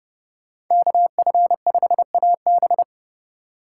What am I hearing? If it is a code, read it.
Morse code, heard as a keyed tone, about 31 wpm, 700 Hz. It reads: KF5AB
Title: USA Call Signs - 2x Character Spacing - 30wpm